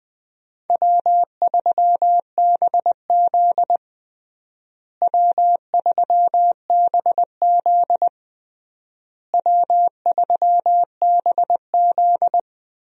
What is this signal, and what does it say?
Morse code, heard as a keyed tone, about 21 wpm, 700 Hz. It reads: W3BZ W3BZ W3BZ
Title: USA Call Signs - 4 Characters - Repeated 3x - 20wpm